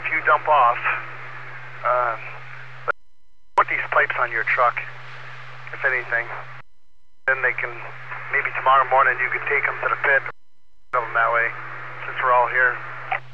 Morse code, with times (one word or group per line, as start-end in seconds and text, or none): none